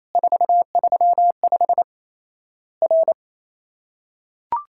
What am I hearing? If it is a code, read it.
Morse code, heard as a keyed tone, about 27 wpm, 700 Hz. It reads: 435 R
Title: Worked All Europe DX Contest - QTC Exchange - 28wpm